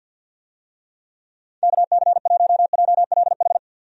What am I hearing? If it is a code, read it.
Morse code, heard as a keyed tone, about 46 wpm, 700 Hz. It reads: KC1JLH
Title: USA Call Signs - 6 Characters - 50wpm